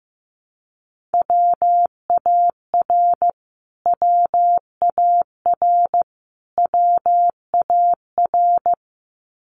Morse code, heard as a keyed tone, about 15 wpm, 700 Hz. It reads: WAR WAR WAR
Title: Top 500 Words Repeated 15wpm